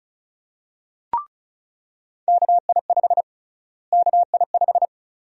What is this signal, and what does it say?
Morse code, heard as a keyed tone, about 36 wpm, 700 Hz. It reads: KI5 KI5
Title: USA Call Sign Prefixes - Repeated 2x - 35wpm